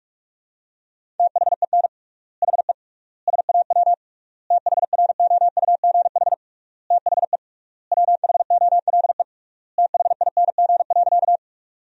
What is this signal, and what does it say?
Morse code, heard as a keyed tone, about 46 wpm, 700 Hz. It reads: THEN HE SAW THROUGH THE WHOLE THING.
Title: 3x Word-Spacing - Sentences from Top 400 Words - 45wpm